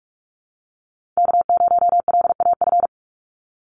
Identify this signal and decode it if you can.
Morse code, heard as a keyed tone, about 46 wpm, 690 Hz. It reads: K0LAF
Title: NA SOTA Chaser Callsigns 45wpm